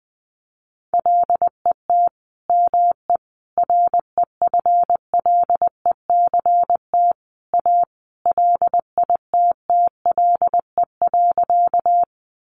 Morse code, heard as a keyed tone, about 20 wpm, 700 Hz. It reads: LET ME REFLECT A LITTLE.
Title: Sentences from Top 1500 Words 20wpm